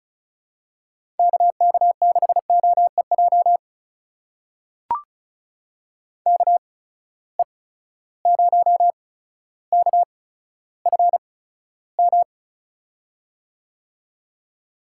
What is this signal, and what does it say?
Morse code, heard as a keyed tone, about 36 wpm, spaced as for 15 wpm, 700 Hz. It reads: KK6OEJ KE0KFM
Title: USA Call Signs - 6 Characters - 8x Character Spacing - 35wpm